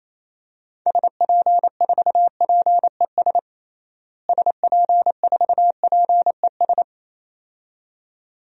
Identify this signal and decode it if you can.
Morse code, heard as a keyed tone, about 28 wpm, 700 Hz. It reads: SP4PES SP4PES
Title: DXCC-100 Call Signs - 6 Characters - Repeated 2x - 28wpm